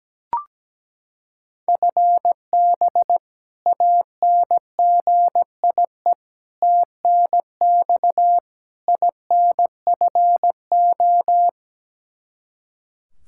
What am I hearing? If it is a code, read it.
Morse code, heard as a keyed tone, about 17 wpm, 700 Hz. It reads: FB ANGIE TNX INFO
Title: K1USN SST QSO Practice 17wpm